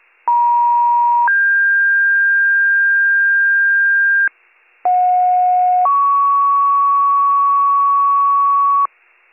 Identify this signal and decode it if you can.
Two-tone page, heard as a one-second tone, then a three-second tone, about 950 then 1600 Hz, about 730 then 1120 Hz, with noise. It is heard from 0.3 to 8.9 s.